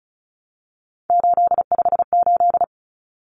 Morse code, heard as a keyed tone, about 34 wpm, 700 Hz. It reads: 858
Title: Numbers - 3-Digit 35wpm